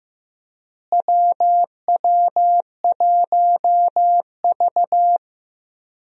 Morse code, heard as a keyed tone, about 15 wpm, 700 Hz. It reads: WW1V
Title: USA Call Signs - 4 Characters - Repeated 2x - 15wpm